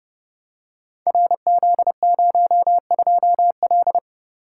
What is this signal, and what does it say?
Morse code, heard as a keyed tone, about 29 wpm, 700 Hz. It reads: RZ02L
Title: Code Groups - 5 Letter:Numbers - 30wpm